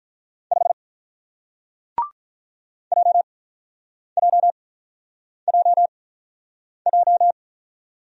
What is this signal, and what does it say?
Morse code, heard as a keyed tone, about 47 wpm, spaced as for 9 wpm, 700 Hz. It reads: 5 JJJJ